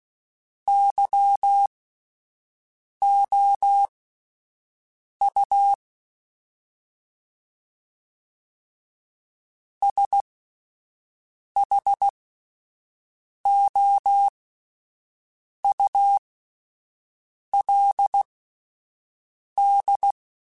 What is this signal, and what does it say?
Morse code, heard as a keyed tone, about 16 wpm, spaced as for 5 wpm, 800 Hz. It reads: YOU SHOULD
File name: Spaced / Chuckles Spaced-mp3s / Temptation sp @10